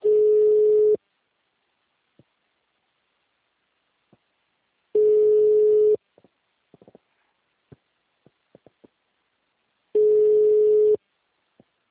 Ringback tone (Europe, 425 Hz).